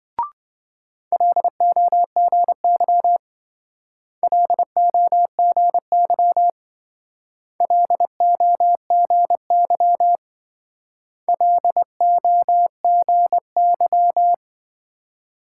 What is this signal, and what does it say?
Morse code, heard as a keyed tone, about 27 wpm, 700 Hz. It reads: LOGY LOGY LOGY LOGY